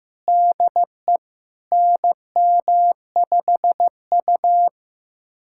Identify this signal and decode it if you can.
Morse code, heard as a keyed tone, about 15 wpm, 700 Hz. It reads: DE NM5U
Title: USA Callsign DE Callsign 15wpm